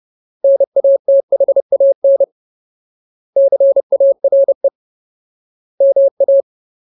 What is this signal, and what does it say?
Morse code, heard as a keyed tone, about 30 wpm, 550 Hz. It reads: NATHAN CARE MA
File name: Ultimate Word List / 30WPM Ultimate